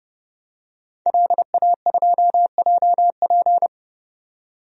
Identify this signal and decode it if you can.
Morse code, heard as a keyed tone, about 30 wpm, 700 Hz. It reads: LA2JP
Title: DXCC-100 Call Signs - 5 Characters - 30wpm